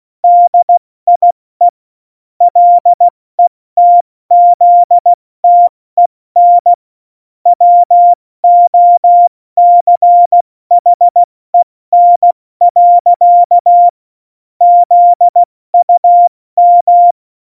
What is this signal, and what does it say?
Morse code, heard as a keyed tone, about 16 wpm, 700 Hz. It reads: DIE LETZTEN WOCHEN. ZUM
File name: Morsekurs / Witze / Witz 05. Januar 2021, 70 F80